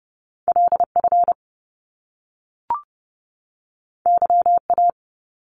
Morse code, heard as a keyed tone, about 30 wpm, 700 Hz. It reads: LF YA
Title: Top Two Letter Combinations in Words 30wpm